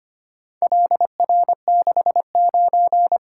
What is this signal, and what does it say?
Morse code, heard as a keyed tone, about 25 wpm, 700 Hz. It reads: LR69
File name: practice / 25wpm / Four Letter-Number Combinations - 25wpm